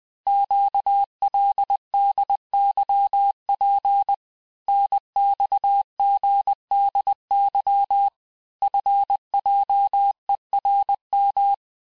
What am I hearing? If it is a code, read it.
Morse code, heard as a keyed tone, about 20 wpm, 800 Hz. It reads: QLDYP NXGDY FJERM
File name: cours-858-texte-chiffre-vitesse-19mpm